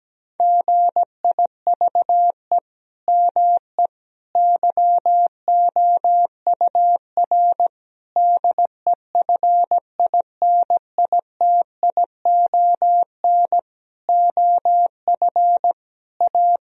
Morse code, heard as a keyed tone, about 17 wpm, 700 Hz. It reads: GIVE ME YOUR DEFINITION OF A
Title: Sentences from Top 1000 Words - Review Part 4 - 17wpm